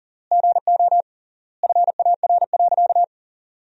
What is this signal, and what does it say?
Morse code, heard as a keyed tone, about 40 wpm, 700 Hz. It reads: GO FAR.
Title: 3x Word-Spacing - Sentences from Top 400 Words - 40wpm